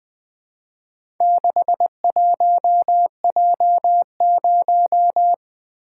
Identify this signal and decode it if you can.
Morse code, heard as a keyed tone, about 20 wpm, 700 Hz. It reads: 61J0
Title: Four Letter-Number Combinations - 20wpm